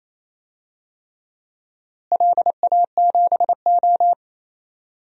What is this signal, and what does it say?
Morse code, heard as a keyed tone, about 28 wpm, 700 Hz. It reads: LA7O
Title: DXCC-100 Call Signs - 4 Characters - Repeated 3x - 28wpm